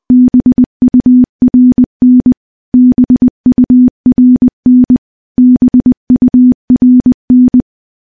Morse code, heard as a keyed tone, about 20 wpm, 250 Hz. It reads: BURN BURN BURN
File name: morse / morse-training-20WPM